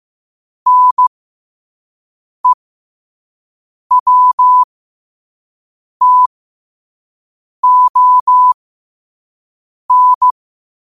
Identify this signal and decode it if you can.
Morse code, heard as a keyed tone, about 15 wpm, spaced as for 5 wpm, 1000 Hz. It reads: NEWTON
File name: qotdfiles / 5wpm